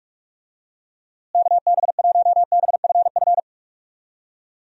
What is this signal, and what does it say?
Morse code, heard as a keyed tone, about 43 wpm, 700 Hz. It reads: KB1BFF
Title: USA Call Signs - 6 Characters - Repeated 3x - 45wpm